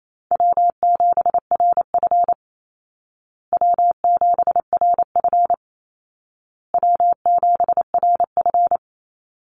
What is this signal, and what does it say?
Morse code, heard as a keyed tone, about 28 wpm, 700 Hz. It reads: W7RF W7RF W7RF
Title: Easy Call Signs Repeated 3x - 28wpm